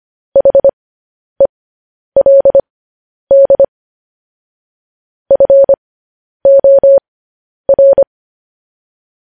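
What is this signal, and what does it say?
Morse code, heard as a keyed tone, about 25 wpm, spaced as for 10 wpm, 550 Hz. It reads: HELD FOR
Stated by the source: Morse code news headlines.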